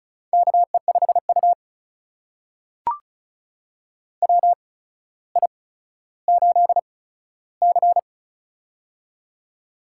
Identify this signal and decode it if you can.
Morse code, heard as a keyed tone, about 36 wpm, spaced as for 15 wpm, 700 Hz. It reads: KE5U WI8C
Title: USA Call Signs - 4 Characters - 8x Character Spacing - 35wpm